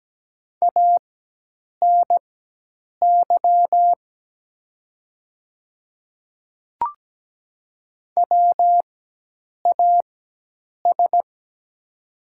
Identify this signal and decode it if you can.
Morse code, heard as a keyed tone, about 17 wpm, spaced as for 8 wpm, 700 Hz. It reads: ANY WAS
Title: 4x Character Spacing - Top 200 Words - 17wpm